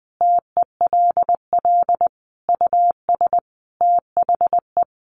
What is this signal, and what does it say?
Morse code, heard as a keyed tone, about 20 wpm, 700 Hz. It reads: TELL US THE
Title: Sentences from Top 1300 Words 20wpm